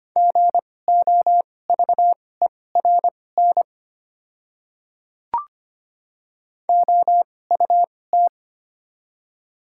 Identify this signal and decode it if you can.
Morse code, heard as a keyed tone, about 25 wpm, spaced as for 18 wpm, 700 Hz. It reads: GOVERN OUT
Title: 2x Character Spacing - Top 500 Words - 25wpm